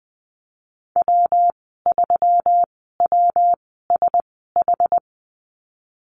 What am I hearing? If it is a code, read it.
Morse code, heard as a keyed tone, about 20 wpm, spaced as for 15 wpm, 700 Hz. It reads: W3WSH
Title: 140 Lesson 23g - Letter f - Call Signs 20wpm